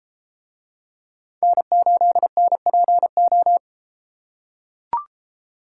Morse code, heard as a keyed tone, about 32 wpm, 700 Hz. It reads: N8NPO